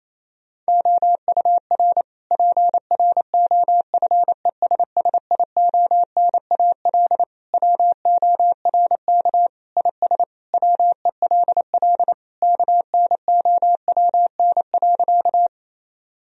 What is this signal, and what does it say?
Morse code, heard as a keyed tone, about 27 wpm, 700 Hz. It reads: OUR PROFESSIONAL WORK IS WELL KNOWN.
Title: Sentences from Top 1000 Words - Review Part 4 - 28wpm